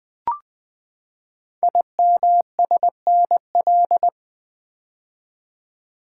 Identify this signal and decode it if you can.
Morse code, heard as a keyed tone, about 21 wpm, 700 Hz. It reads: IMSNL
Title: Code Groups - 5 Letters - 20wpm